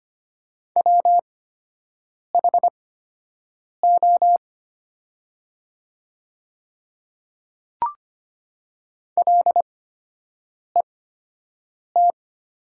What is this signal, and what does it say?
Morse code, heard as a keyed tone, about 25 wpm, spaced as for 7 wpm, 690 Hz. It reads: WHO LET
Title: 8x Character Spacing - Top 300 Words - 25wpm